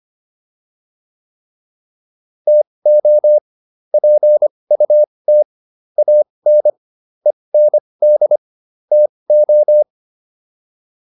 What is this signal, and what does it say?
Morse code, heard as a keyed tone, about 25 wpm, spaced as for 20 wpm, 600 Hz. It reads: TO PUT AN END TO